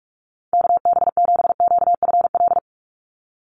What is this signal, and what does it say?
Morse code, heard as a keyed tone, about 44 wpm, 700 Hz. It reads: KB7QFL
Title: USA Call Signs 45wpm